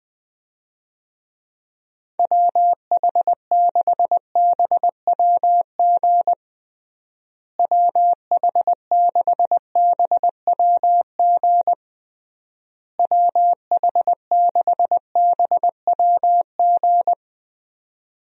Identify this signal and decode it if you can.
Morse code, heard as a keyed tone, about 20 wpm, 700 Hz. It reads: WH6BWG WH6BWG WH6BWG
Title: USA Call Signs - 6 Characters - Repeated 3x - 20wpm